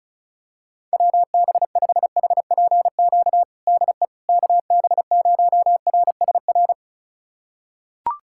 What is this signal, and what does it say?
Morse code, heard as a keyed tone, about 36 wpm, 700 Hz. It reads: WB5HPQ DE KB0RSR
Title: USA Callsign DE Callsign - 6 Characters - 35wpm